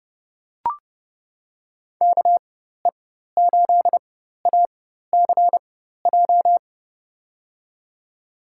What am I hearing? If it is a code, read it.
Morse code, heard as a keyed tone, about 30 wpm, spaced as for 14 wpm, 700 Hz. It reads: KE8ACJ
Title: USA Call Signs - 6 Characters - 4x Character Spacing - 30wpm